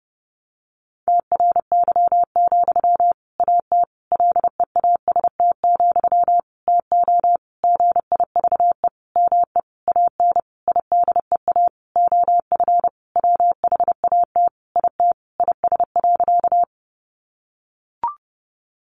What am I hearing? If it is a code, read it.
Morse code, heard as a keyed tone, about 30 wpm, 700 Hz. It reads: TRY, AT LEAST, TO GIVE ME AN IDEA OF WHAT IT IS.
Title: Sentences from Top 700 Words 30wpm